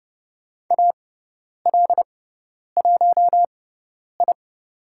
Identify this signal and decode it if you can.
Morse code, heard as a keyed tone, about 30 wpm, spaced as for 10 wpm, 700 Hz. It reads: AL1I
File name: Lesson 16g - Letter c - Call Signs 30wpm